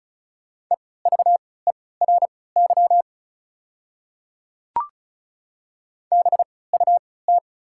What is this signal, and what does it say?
Morse code, heard as a keyed tone, about 36 wpm, spaced as for 20 wpm, 700 Hz. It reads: EVERY BUT
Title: 3x Character Spacing - Top 300 Words - 35wpm